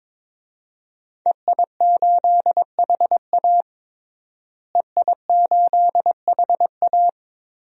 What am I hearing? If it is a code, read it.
Morse code, heard as a keyed tone, about 21 wpm, 700 Hz. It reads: EI8HA EI8HA
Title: DXCC-100 Call Signs - 5 Characters - Repeated 3x - 22wpm